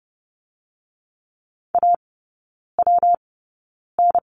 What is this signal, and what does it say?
Morse code, heard as a keyed tone, about 30 wpm, spaced as for 9 wpm, 700 Hz. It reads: AWN
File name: Lesson 19e - Letter w - Three Characters 30wpm